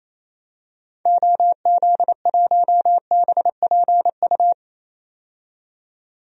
Morse code, heard as a keyed tone, about 28 wpm, 700 Hz. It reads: OZ1BPU
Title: DXCC-100 Call Signs - 6 Characters - 28wpm